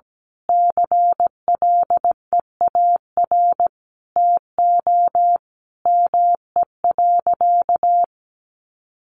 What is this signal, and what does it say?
Morse code, heard as a keyed tone, about 17 wpm, 700 Hz. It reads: CLEAR TO ME.
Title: Sentences from Top 1300 Words 17wpm